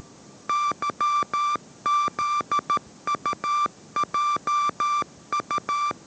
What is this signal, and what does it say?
Morse code, heard as a keyed tone, about 13 wpm, 1180 Hz. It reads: YZUJU
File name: 14wpm